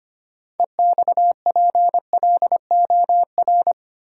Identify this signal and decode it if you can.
Morse code, heard as a keyed tone, about 25 wpm, 700 Hz. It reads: EXPLOR
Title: Sentences from Top 1100 Words - New Word Drill - 25wpm